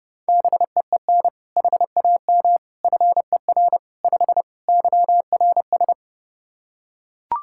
Morse code, heard as a keyed tone, about 30 wpm, 700 Hz. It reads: BEEN HAM FER 5 YRS
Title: K1USN SST QSO Practice 30wpm